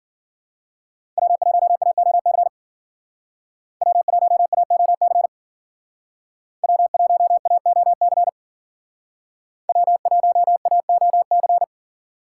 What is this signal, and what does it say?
Morse code, heard as a keyed tone, about 52 wpm, 700 Hz. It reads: W1AOC W1AOC W1AOC W1AOC